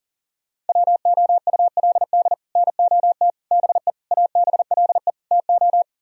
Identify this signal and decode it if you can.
Morse code, heard as a keyed tone, about 46 wpm, 700 Hz. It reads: WOULD NOT BE ABLE TO
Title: Speed-Racing - Sets of 5 Words 40wpm